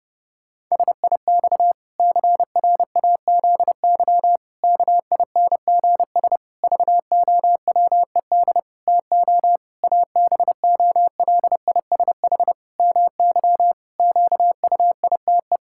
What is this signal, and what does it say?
Morse code, heard as a keyed tone, about 30 wpm, 700 Hz. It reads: SIX CRAZY KINGS VOWED TO ABOLISH MY QUITE
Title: Pangram Challenge v2 - 30wpm